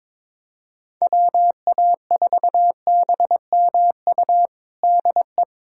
Morse code, heard as a keyed tone, about 22 wpm, 700 Hz. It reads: WA4BMU DE
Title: USA Callsign DE Callsign 22wpm